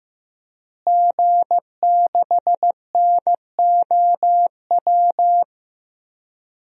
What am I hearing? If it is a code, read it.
Morse code, heard as a keyed tone, about 15 wpm, 700 Hz. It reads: G6NOW
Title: DXCC-100 Call Signs - 5 Characters - Repeated 3x - 15wpm